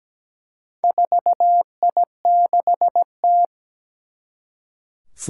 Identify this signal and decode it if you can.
Morse code, heard as a keyed tone, about 17 wpm, 700 Hz. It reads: 4I6T
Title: Four Challenging Letter-Number Combinations 17wpm